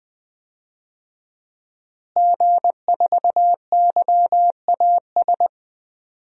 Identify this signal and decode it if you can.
Morse code, heard as a keyed tone, about 20 wpm, 700 Hz. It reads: G4YAS
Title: DXCC-100 Call Signs - 5 Characters - Repeated 2x - 20wpm